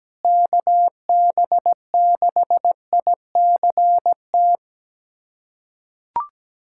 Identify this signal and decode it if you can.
Morse code, heard as a keyed tone, about 17 wpm, 700 Hz. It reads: KB6ICT
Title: USA Callsign DE Callsign 17wpm